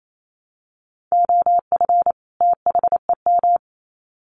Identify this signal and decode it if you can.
Morse code, heard as a keyed tone, about 28 wpm, 700 Hz. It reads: OF THEM
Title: Sets of 2 Words 28wpm